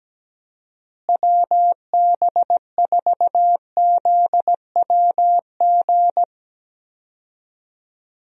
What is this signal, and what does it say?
Morse code, heard as a keyed tone, about 17 wpm, 700 Hz. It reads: WB4ZWG
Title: USA Call Signs - 6 Characters - Repeated 3x - 17wpm